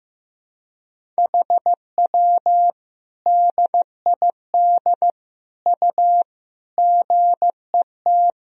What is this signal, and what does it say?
Morse code, heard as a keyed tone, about 16 wpm, 700 Hz. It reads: HW DID U GET
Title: Sentences from Top 200 Words and Abbreviations 15wpm